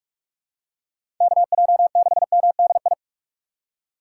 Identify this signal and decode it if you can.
Morse code, heard as a keyed tone, about 46 wpm, 700 Hz. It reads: KJ6MDI
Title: USA Call Signs - 6 Characters - 3x Character Spacing - 45wpm